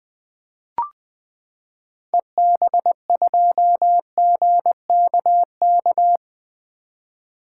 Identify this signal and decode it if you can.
Morse code, heard as a keyed tone, about 20 wpm, 700 Hz. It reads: EB2GKK
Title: 2023 SOTA Activator Call Signs - 20wpm